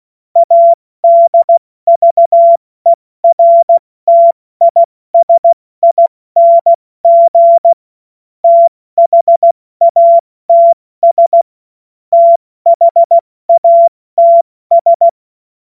Morse code, heard as a keyed tone, about 16 wpm, 680 Hz. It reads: ADVERTISING THATS THATS
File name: Morsekurs / Witze / Witz 27. September 2022 70 F80